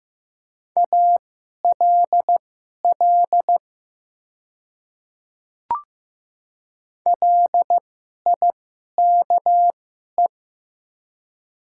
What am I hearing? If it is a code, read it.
Morse code, heard as a keyed tone, about 15 wpm, spaced as for 11 wpm, 700 Hz. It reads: ALL LIKE
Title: 2x Character Spacing - Top 100 Words - 15wpm